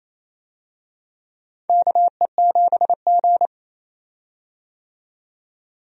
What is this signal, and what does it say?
Morse code, heard as a keyed tone, about 28 wpm, 700 Hz. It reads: KE7G
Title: USA Call Signs - 4 Characters - 28wpm